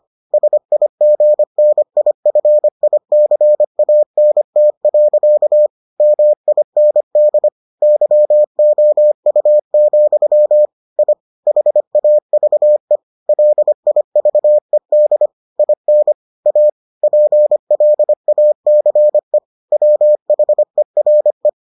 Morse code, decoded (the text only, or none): SIGNIFICANT. MIND YOU, I HAVE LIVED IN A PLACE WHERE